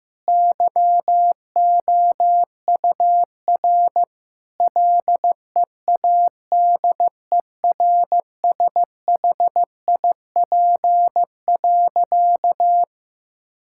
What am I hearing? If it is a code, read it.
Morse code, heard as a keyed tone, about 16 wpm, 700 Hz. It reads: YOUR LEADERSHIP.
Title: Sentences from Top 1600 Words 15wpm